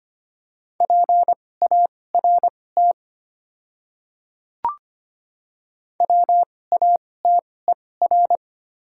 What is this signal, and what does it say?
Morse code, heard as a keyed tone, about 25 wpm, spaced as for 18 wpm, 700 Hz. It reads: PART WATER
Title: 2x Character Spacing - Top 300 Words - 25wpm